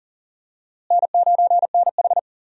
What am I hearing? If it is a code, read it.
Morse code, heard as a keyed tone, about 39 wpm, 700 Hz. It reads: N9NH
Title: USA Call Signs - 4 Characters - Repeated 3x - 40wpm